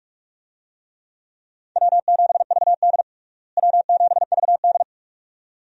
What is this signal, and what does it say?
Morse code, heard as a keyed tone, about 46 wpm, 700 Hz. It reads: W7VD W7VD
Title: USA Call Signs - 4 Characters - Repeated 2x - 45wpm